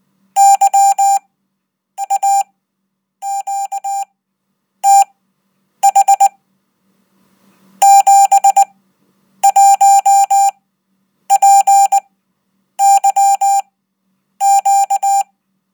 Morse code, heard as a keyed tone, about 19 wpm, spaced as for 8 wpm, 790 Hz. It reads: YAQTH 71PYQ
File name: Code